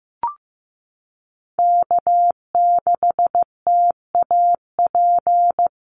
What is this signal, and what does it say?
Morse code, heard as a keyed tone, about 15 wpm, 700 Hz. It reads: K6TAP
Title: USA Call Signs Repeated 2x 15wpm